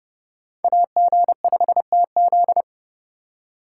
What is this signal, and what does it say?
Morse code, heard as a keyed tone, about 31 wpm, 700 Hz. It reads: AG5TZ